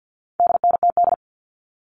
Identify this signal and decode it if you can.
Morse code, heard as a keyed tone, about 50 wpm, 700 Hz. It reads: DNTD
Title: All Cut Numbers - 4-Digit 50wpm